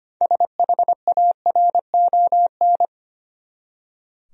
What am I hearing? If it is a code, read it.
Morse code, heard as a keyed tone, about 25 wpm, 700 Hz. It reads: SHARON
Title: USA Common Names - 25wpm